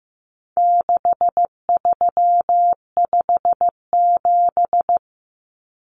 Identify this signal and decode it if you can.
Morse code, heard as a keyed tone, about 15 wpm, 700 Hz. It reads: 6357
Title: Common Cut Numbers - 4-Digit 15wpm